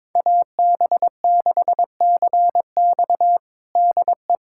Morse code, heard as a keyed tone, about 22 wpm, 700 Hz. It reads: AB6CX DE
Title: USA Callsign DE Callsign - 5 Characters - 22wpm